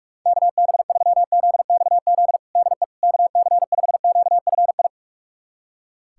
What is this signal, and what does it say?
Morse code, heard as a keyed tone, about 46 wpm, 700 Hz. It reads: KB3ZXZ DE KC5QFI
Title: USA Callsign DE Callsign - 6 Characters - 45wpm